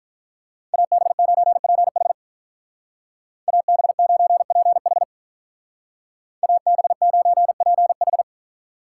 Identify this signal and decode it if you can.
Morse code, heard as a keyed tone, about 47 wpm, 700 Hz. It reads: AB9PH AB9PH AB9PH